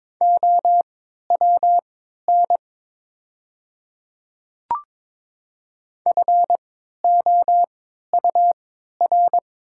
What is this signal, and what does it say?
Morse code, heard as a keyed tone, about 22 wpm, spaced as for 12 wpm, 700 Hz. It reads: OWN FOUR
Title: 3x Character Spacing - Top 500 Words - 22wpm